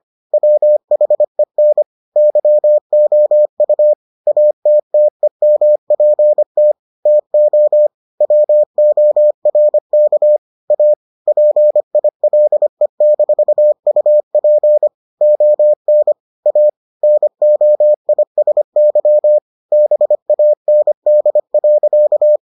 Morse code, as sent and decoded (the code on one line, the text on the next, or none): .-- .... -. / -.-- --- ..- / .- - - . -- .--. - / - --- / .-- --- .-. -.- / .- / .--. .. .-.. -....- ..- .--. / --- -. / .- / -. --- .. ... -.-- / -... .- -. -.. .-.-.-
WHEN YOU ATTEMPT TO WORK A PILE-UP ON A NOISY BAND.